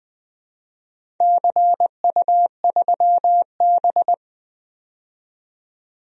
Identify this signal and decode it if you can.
Morse code, heard as a keyed tone, about 20 wpm, 700 Hz. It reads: CU3B
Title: DXCC-100 Call Signs - 4 Characters - 20wpm